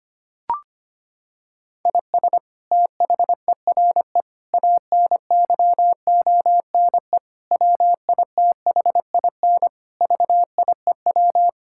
Morse code, heard as a keyed tone, about 25 wpm, 700 Hz. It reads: IS THERE ANYONE WITHIN VIEW
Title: Sentences from Top 1000 Words - Review Part 6 - 25wpm